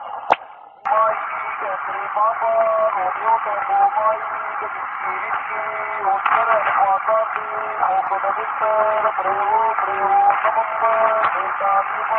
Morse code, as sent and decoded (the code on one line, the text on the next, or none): none